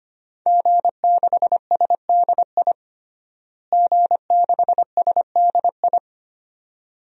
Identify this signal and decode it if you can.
Morse code, heard as a keyed tone, about 25 wpm, 700 Hz. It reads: G6SDI G6SDI
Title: DXCC-100 Call Signs - 5 Characters - Repeated 2x - 25wpm